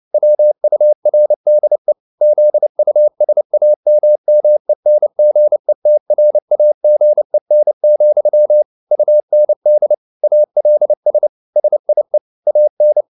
Morse code, heard as a keyed tone, about 29 wpm, 600 Hz. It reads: WURDE ZUSAMMENGETRAGEN, UND ALS SIE AN